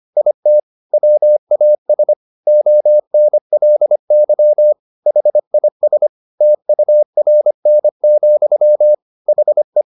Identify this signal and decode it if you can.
Morse code, heard as a keyed tone, about 25 wpm, 600 Hz. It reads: IT WAS ONLY HIS TURN, HE
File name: the old man and the sea part 1 25wpm